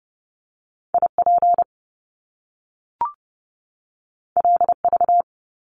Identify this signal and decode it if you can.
Morse code, heard as a keyed tone, about 30 wpm, 700 Hz. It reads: IP LV
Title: Top Two Letter Combinations in Words 30wpm